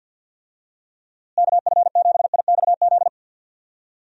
Morse code, heard as a keyed tone, about 46 wpm, 700 Hz. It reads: KF7IXZ
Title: USA Call Signs - 6 Characters - Repeated 2x - 50wpm